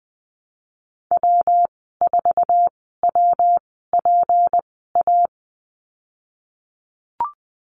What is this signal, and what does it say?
Morse code, heard as a keyed tone, about 20 wpm, spaced as for 15 wpm, 700 Hz. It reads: W4WPA
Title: 167 Lesson 27g - Number 7 - Call Signs 20wpm